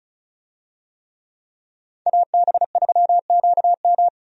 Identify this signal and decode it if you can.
Morse code, heard as a keyed tone, about 34 wpm, 700 Hz. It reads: AB3QM
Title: USA Call Signs - 5 Characters - 35wpm